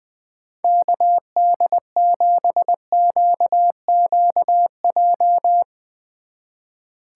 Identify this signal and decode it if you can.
Morse code, heard as a keyed tone, about 21 wpm, 700 Hz. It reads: KD7QQJ
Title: Speed Racing - USA Call Signs 20wpm